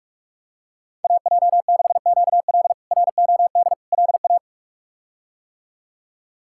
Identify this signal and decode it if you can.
Morse code, heard as a keyed tone, about 47 wpm, 700 Hz. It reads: AJ6QL ROD LA
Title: K1USN SST QSO Practice - 45wpm